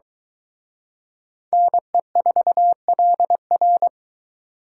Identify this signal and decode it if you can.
Morse code, heard as a keyed tone, about 23 wpm, 700 Hz. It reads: NE4LR